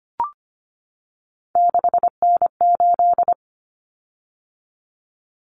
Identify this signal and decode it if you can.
Morse code, heard as a keyed tone, about 25 wpm, 700 Hz. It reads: 6N8
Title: Common Cut Numbers - 3-Digit 25wpm